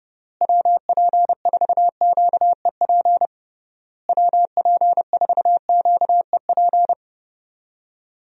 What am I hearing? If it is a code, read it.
Morse code, heard as a keyed tone, about 30 wpm, 700 Hz. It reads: WP4QEP WP4QEP
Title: USA Call Signs - 6 Characters - Repeated 3x - 30wpm